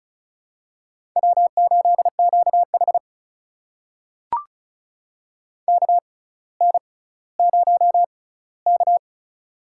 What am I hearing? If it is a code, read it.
Morse code, heard as a keyed tone, about 34 wpm, spaced as for 18 wpm, 700 Hz. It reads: W8QH KN0K
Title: USA Call Signs - 4 Characters - 6x Character Spacing - 35wpm